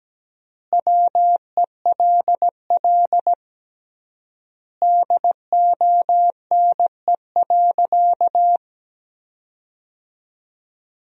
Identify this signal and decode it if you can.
Morse code, heard as a keyed tone, about 17 wpm, 700 Hz. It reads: WELL DONE.
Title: 3x Word-Spacing - Sentences from Top 500 Words - 17wpm